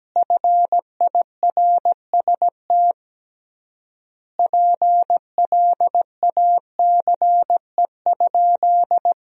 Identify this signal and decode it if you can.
Morse code, heard as a keyed tone, about 17 wpm, 700 Hz. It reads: FIRST PLACE?
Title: 3x Word-Spacing - Sentences from Top 200 Words - 17wpm